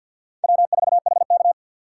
Morse code, heard as a keyed tone, about 50 wpm, 700 Hz. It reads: W4HK
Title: USA Call Signs - 4 Characters - 6x Character Spacing - 50wpm